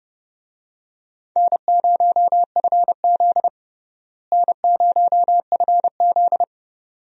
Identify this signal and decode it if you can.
Morse code, heard as a keyed tone, about 32 wpm, 700 Hz. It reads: N0FZ N0FZ
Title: USA Call Signs - 4 Characters - Repeated 2x - 30wpm